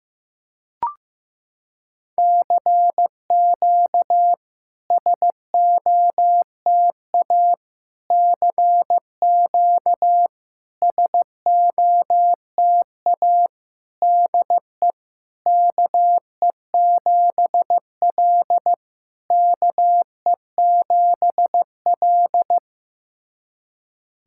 Morse code, heard as a keyed tone, about 15 wpm, 700 Hz. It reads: CQ SOTA CQ SOTA DE KE7L KE7L
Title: SOTA QSO Practice 15wpm